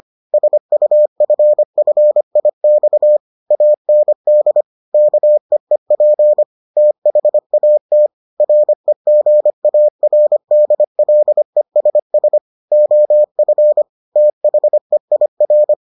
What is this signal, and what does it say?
Morse code, heard as a keyed tone, about 25 wpm, 600 Hz. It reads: SUFFIX AND KEEP THAT REGARDLESS OF THEIR